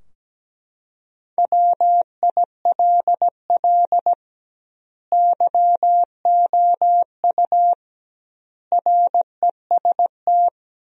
Morse code, heard as a keyed tone, about 17 wpm, 700 Hz. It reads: WILL YOU REST
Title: 2x Word-Spacing - Sentences from Top 500 Words - 17wpm